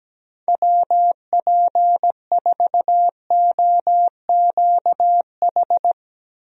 Morse code, heard as a keyed tone, about 17 wpm, 700 Hz. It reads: WP4OQH
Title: USA Call Signs - 6 Characters - Repeated 2x - 17wpm